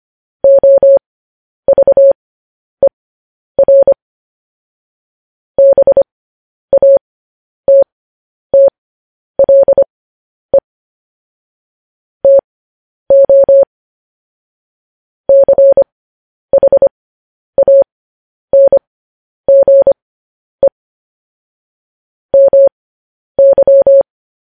Morse code, heard as a keyed tone, about 25 wpm, spaced as for 10 wpm, 550 Hz. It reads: OVER BATTLE TO CHANGE MY
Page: news-headlines-in-morse-code-at-10-wpm